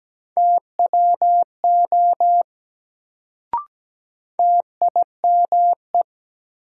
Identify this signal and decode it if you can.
Morse code, heard as a keyed tone, about 17 wpm, 700 Hz. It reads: TWO TIME
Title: Sets of 2 Words - New Word Drill - 17wpm